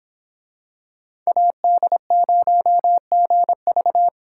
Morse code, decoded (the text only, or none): AD0GV